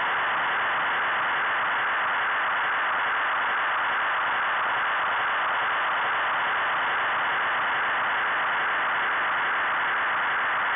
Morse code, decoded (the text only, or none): none